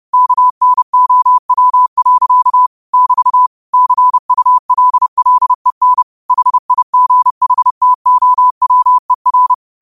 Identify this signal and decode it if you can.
Morse code, heard as a keyed tone, about 30 wpm, 1000 Hz. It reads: KNOW. <BT> CULLEN HIGHTOWER
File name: qotdfiles / 30wpm